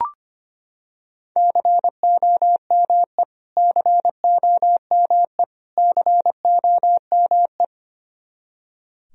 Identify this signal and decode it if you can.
Morse code, heard as a keyed tone, about 25 wpm, 700 Hz. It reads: COME COME COME
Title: Top 100 Words Repeated 25wpm